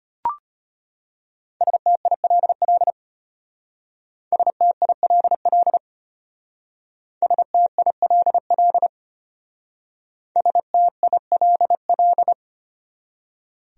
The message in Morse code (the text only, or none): STILL STILL STILL STILL